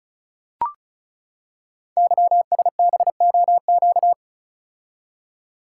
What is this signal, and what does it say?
Morse code, heard as a keyed tone, about 36 wpm, 700 Hz. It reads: YSBOQ
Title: Code Groups - 5 Letters - 35wpm